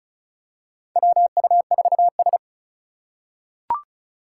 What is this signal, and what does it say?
Morse code, heard as a keyed tone, about 34 wpm, 700 Hz. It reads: WU4S
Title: USA Call Signs - 4 Characters - 2x Character Spacing - 35wpm